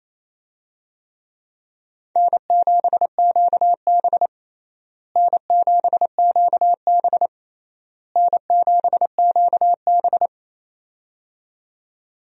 Morse code, heard as a keyed tone, about 27 wpm, 700 Hz. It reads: N7QB N7QB N7QB
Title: USA Call Signs - 4 Characters - Repeated 3x - 28wpm